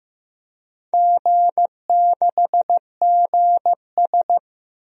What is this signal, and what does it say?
Morse code, heard as a keyed tone, about 15 wpm, 700 Hz. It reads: G6GS
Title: DXCC-100 Call Signs - 4 Characters - 15wpm